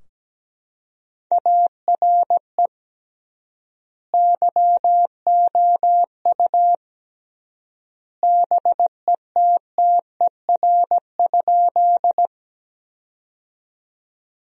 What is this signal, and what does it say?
Morse code, heard as a keyed tone, about 17 wpm, 700 Hz. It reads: ARE YOU BETTER?
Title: 3x Word-Spacing - Sentences from Top 400 Words - 17wpm